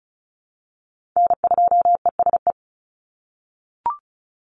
Morse code, heard as a keyed tone, about 34 wpm, 700 Hz. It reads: N2ESE
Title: NA SOTA Chaser Callsigns 35wpm